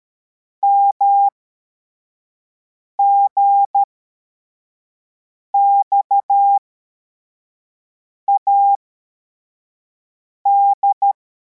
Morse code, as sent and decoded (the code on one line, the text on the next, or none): -- --. -..- .- -..
MGXAD